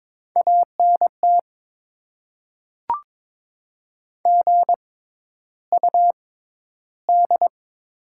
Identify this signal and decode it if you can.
Morse code, heard as a keyed tone, about 22 wpm, spaced as for 11 wpm, 690 Hz. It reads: ANT GUD
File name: Top 20 QSO Elements - 6x Character Spacing - 22wpm